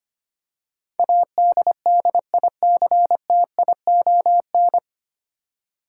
Morse code, heard as a keyed tone, about 25 wpm, 700 Hz. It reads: ADDICTION
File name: Nine Letter Words - 25wpm